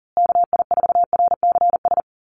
Morse code, heard as a keyed tone, about 40 wpm, 700 Hz. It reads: KI4RCS
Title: USA Call Signs Repeated 2x 40wpm